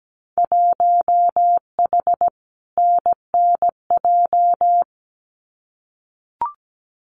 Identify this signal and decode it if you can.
Morse code, heard as a keyed tone, about 17 wpm, 700 Hz. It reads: 1H NNJ
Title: Winter Field Day Exchanges 17wpm